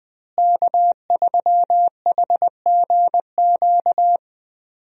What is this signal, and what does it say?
Morse code, heard as a keyed tone, about 20 wpm, 700 Hz. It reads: K3HGQ
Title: USA Call Signs - 5 Characters - Repeated 3x - 20wpm